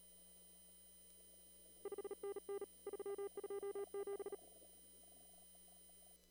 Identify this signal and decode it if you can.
Morse code, heard as a keyed tone, about 38 wpm, 1200 Hz. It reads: 5NN 327
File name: CW / kt5j wpx cw-op-k5pi